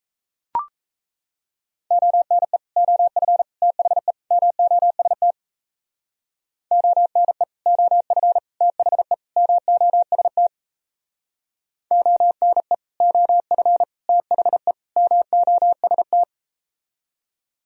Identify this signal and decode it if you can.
Morse code, heard as a keyed tone, about 37 wpm, 700 Hz. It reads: ONE OF THE MOST ONE OF THE MOST ONE OF THE MOST